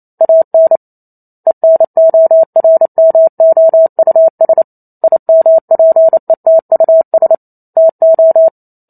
Morse code, heard as a keyed tone, about 28 wpm, 650 Hz. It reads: AN ENORMOUS IMPETUS TO